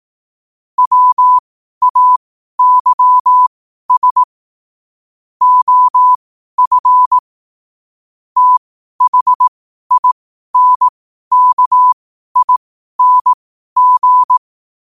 Morse code, heard as a keyed tone, about 18 wpm, spaced as for 12 wpm, 1000 Hz. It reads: WAYS OF THINKING
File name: qotdfiles / 12wpm